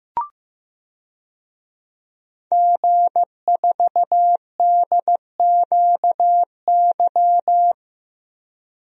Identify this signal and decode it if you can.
Morse code, heard as a keyed tone, about 15 wpm, 700 Hz. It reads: G4DQY
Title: DXCC-100 Call Signs - 5 Characters - Repeated 2x - 15wpm